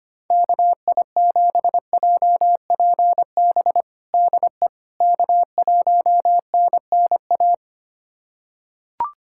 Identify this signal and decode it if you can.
Morse code, heard as a keyed tone, about 25 wpm, 700 Hz. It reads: KI7JPB DE K1NNA
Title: USA Callsign DE Callsign 25wpm